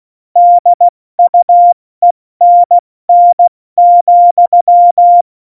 Morse code, heard as a keyed tone, about 16 wpm, spaced as for 14 wpm, 700 Hz. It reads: DUENN,
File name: Morsekurs / Witze / Witz 23. April 2019, 70 F80